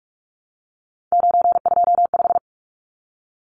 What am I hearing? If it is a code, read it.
Morse code, heard as a keyed tone, about 44 wpm, 700 Hz. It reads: 925
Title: Numbers - 3-Digit 45wpm